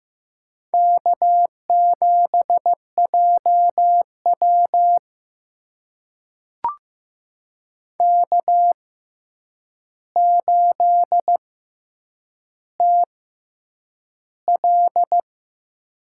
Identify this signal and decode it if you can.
Morse code, heard as a keyed tone, about 15 wpm, spaced as for 8 wpm, 700 Hz. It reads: K7JW K8TL
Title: USA Call Signs - 4 Characters - 6x Character Spacing - 15wpm